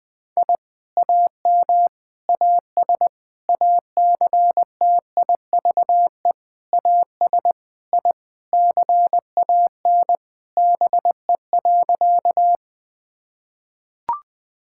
Morse code, heard as a keyed tone, about 20 wpm, 700 Hz. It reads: I AM AS ACTIVE AS I CAN BE.
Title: Sentences from Top 1000 Words - Review Part 6 - 20wpm